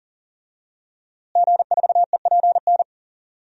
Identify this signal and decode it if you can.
Morse code, heard as a keyed tone, about 40 wpm, 700 Hz. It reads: G4EPN
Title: DXCC-100 Call Signs - 5 Characters - Repeated 2x - 40wpm